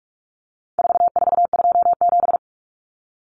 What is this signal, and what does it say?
Morse code, heard as a keyed tone, about 44 wpm, 700 Hz. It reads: 4427
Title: Numbers - 4-Digit 45wpm